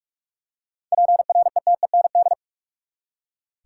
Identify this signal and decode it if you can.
Morse code, heard as a keyed tone, about 43 wpm, 700 Hz. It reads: PRETEND
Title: Seven Letter Words - 45wpm